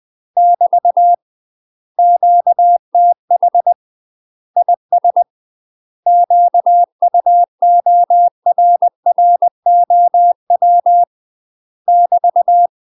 Morse code, heard as a keyed tone, about 20 wpm, 700 Hz. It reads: <BT> QTH IS QUORROW <BT>
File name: UK MASTER QSOs TRAINING / 20WPM / UK MASTER QSOs 0023